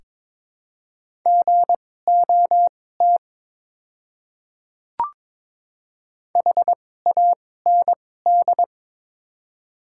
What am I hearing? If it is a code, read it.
Morse code, heard as a keyed tone, about 22 wpm, spaced as for 16 wpm, 700 Hz. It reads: GOT HAND
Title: 2x Character Spacing - Top 300 Words - 22wpm